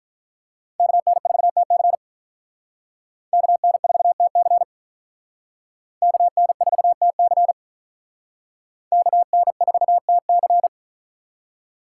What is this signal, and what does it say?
Morse code, heard as a keyed tone, about 44 wpm, 700 Hz. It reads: KN4TC KN4TC KN4TC KN4TC